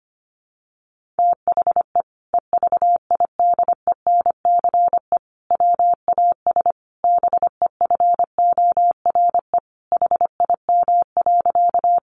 Morse code, heard as a keyed tone, about 25 wpm, 700 Hz. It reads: THE EVIDENCE WAS BEFORE HIM.
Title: Sentences from Top 1300 Words 25wpm